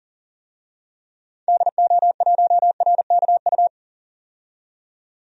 Morse code, heard as a keyed tone, about 40 wpm, 700 Hz. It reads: DO1RKU
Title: DXCC-100 Call Signs - 6 Characters - 40wpm